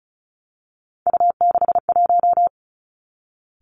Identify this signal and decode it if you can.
Morse code, heard as a keyed tone, about 36 wpm, 700 Hz. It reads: U61
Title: Three Challenging Letter-Number Combinations 35wpm